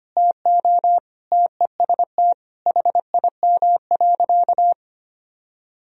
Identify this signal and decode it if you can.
Morse code, heard as a keyed tone, about 25 wpm, 700 Hz. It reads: TO TEST HIM.
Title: Sentences from Top 1000 Words - Review Part 3 - 25wpm